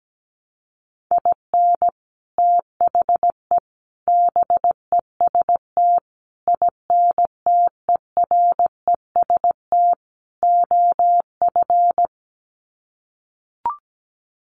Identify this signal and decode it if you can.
Morse code, heard as a keyed tone, about 17 wpm, 700 Hz. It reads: IN THE BEST INTEREST OF
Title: Sets of 5 Words Encore 17wpm